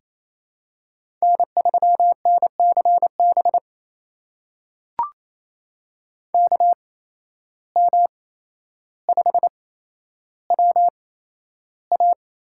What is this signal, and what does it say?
Morse code, heard as a keyed tone, about 27 wpm, spaced as for 12 wpm, 700 Hz. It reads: N3NCB KM5WA
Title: USA Call Signs - 5 Characters - 8x Character Spacing - 28wpm